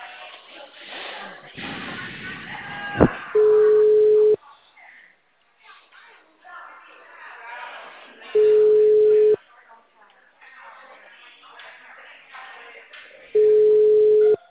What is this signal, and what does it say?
Ringback tone (Europe, 425 Hz), with speech.